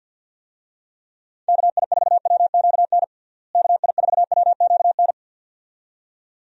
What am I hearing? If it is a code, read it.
Morse code, heard as a keyed tone, about 51 wpm, 700 Hz. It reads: KI4WQN KI4WQN
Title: USA Call Signs - 6 Characters - Repeated 2x - 50wpm